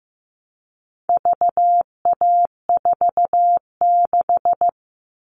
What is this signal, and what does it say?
Morse code, heard as a keyed tone, about 15 wpm, 700 Hz. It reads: VA46
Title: All Cut Numbers - 4-Digit 15wpm